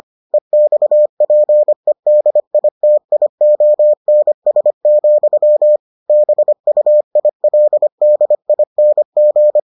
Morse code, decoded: EXPEDITIONS, BUILDING

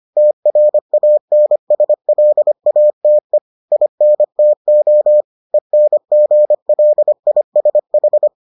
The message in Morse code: TRANSLATE INTO ENGLISH

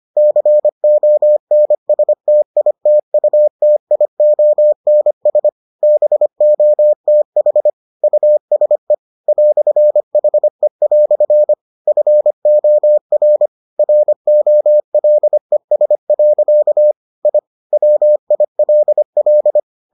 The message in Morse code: CONSTITUTIONS BOTH USE "HE" FOR ROLES. I WILL

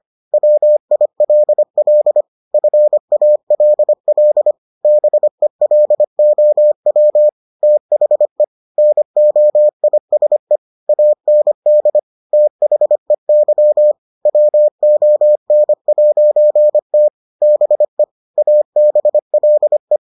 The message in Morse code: WILL FALL BELOW THE NOISE AND THEY WON'T BE ABLE

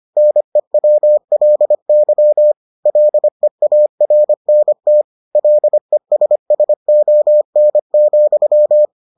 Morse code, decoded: NEWLY LEARNT LESSON,

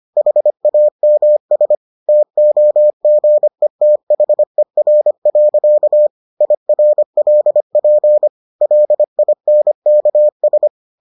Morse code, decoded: HAMS TOGETHER. IRLP LINKS

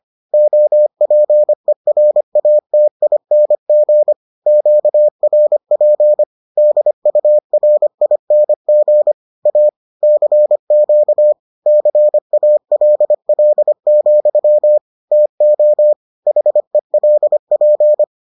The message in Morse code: OPERATING QRP DURING A CQ CALL, TO HELP

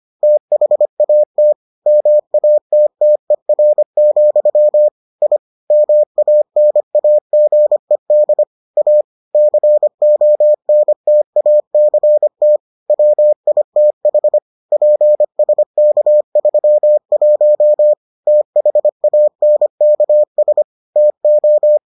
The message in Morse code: THAT MATTER, I MANAGED A CONTACT WITH PSK31 THANKS TO

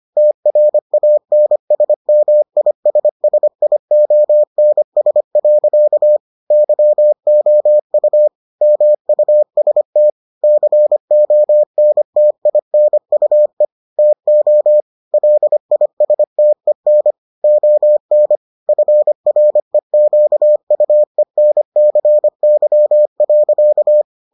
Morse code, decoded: TRANSMISSIONS. YOU MUST CONTINUE TO LISTEN ON FREQUENCY.

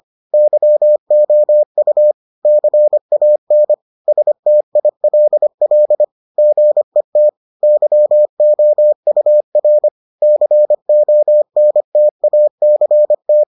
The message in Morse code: YOU CAN STILL GET YOUR CONTACT

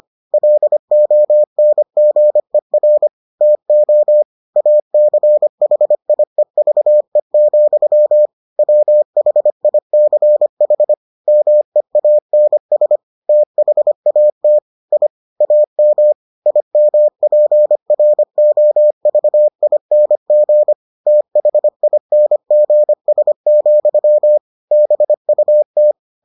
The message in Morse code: LONGER TO ACHIEVE, WHICH MEANS THAT I AM IMPROVING THINGS, BUT

